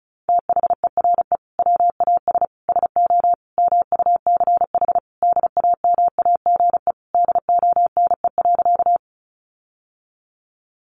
Morse code, heard as a keyed tone, about 34 wpm, 700 Hz. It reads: THERE WAS SO MUCH DAMAGE DONE.
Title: Sentences from Top 1100 Words 35wpm